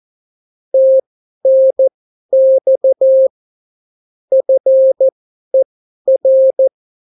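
Morse code, heard as a keyed tone, about 14 wpm, spaced as for 11 wpm, 530 Hz. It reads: TNX FER
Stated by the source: Example Spanish CW QSO from a Morse learning book, 14 wpm characters with 8 wpm spacing.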